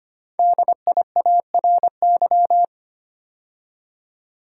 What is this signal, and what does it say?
Morse code, heard as a keyed tone, about 25 wpm, 700 Hz. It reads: DIARY
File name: 5 Letter Words - First 3 Letters Given - 25wpm